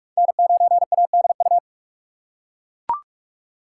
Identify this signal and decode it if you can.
Morse code, heard as a keyed tone, about 41 wpm, 700 Hz. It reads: N9ADU
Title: USA Call Signs - 5 Characters - 2x Character Spacing - 45wpm